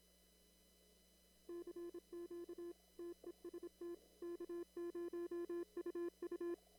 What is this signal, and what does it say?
Morse code, heard as a keyed tone, about 26 wpm, 340 Hz. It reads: CQ TEST K0UU
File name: CW / kt5j wpx cw-op-k5pi